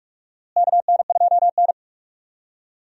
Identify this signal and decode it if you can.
Morse code, heard as a keyed tone, about 46 wpm, 700 Hz. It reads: KN2N
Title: USA Call Signs - 4 Characters - Repeated 3x - 45wpm